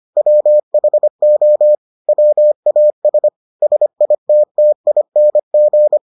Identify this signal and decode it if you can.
Morse code, heard as a keyed tone, about 25 wpm, 600 Hz. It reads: WHO WAS SITTING